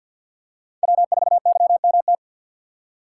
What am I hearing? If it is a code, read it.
Morse code, heard as a keyed tone, about 47 wpm, 700 Hz. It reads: W4YMT
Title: USA Call Signs - 3x Character Spacing - 50wpm